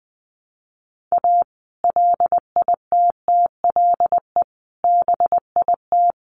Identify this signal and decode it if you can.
Morse code, heard as a keyed tone, about 20 wpm, 700 Hz. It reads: A LITTLE BIT
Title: Sets of 5 Words Encore2 20wpm